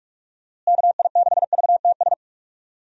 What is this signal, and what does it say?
Morse code, heard as a keyed tone, about 41 wpm, 700 Hz. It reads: KI6VTS
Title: USA Call Signs - 6 Characters - Repeated 2x - 45wpm